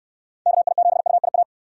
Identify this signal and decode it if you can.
Morse code, heard as a keyed tone, about 66 wpm, 700 Hz. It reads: KI7UIA